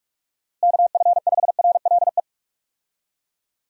Morse code, heard as a keyed tone, about 42 wpm, 700 Hz. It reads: KF5RLE
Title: USA Call Signs - 6 Characters - Repeated 3x - 45wpm